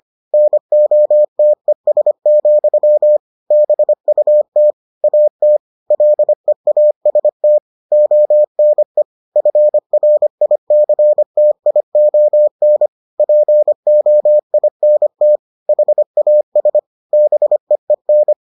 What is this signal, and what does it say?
Morse code, heard as a keyed tone, about 25 wpm, 600 Hz. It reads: NOTES, BUT AT LEAST ONE FRICTION POINT HAS BEEN